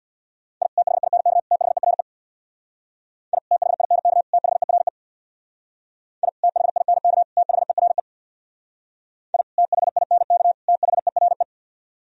Morse code, heard as a keyed tone, about 64 wpm, 700 Hz. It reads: I THINK THERE I THINK THERE I THINK THERE I THINK THERE